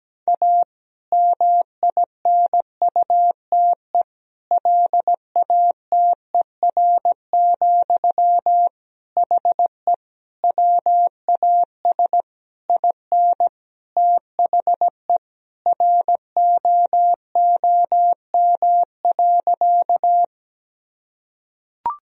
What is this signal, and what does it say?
Morse code, heard as a keyed tone, about 17 wpm, 700 Hz. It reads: A MINUTE LATER, HE WAS IN THE ROOM.
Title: Sentences from Top 1000 Words - Review Part 4 - 17wpm